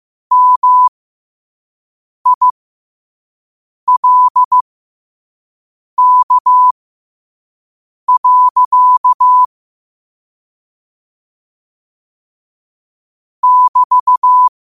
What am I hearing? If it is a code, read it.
Morse code, heard as a keyed tone, about 15 wpm, spaced as for 5 wpm, 1000 Hz. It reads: MILK. <BT>
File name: qotdfiles / 5wpm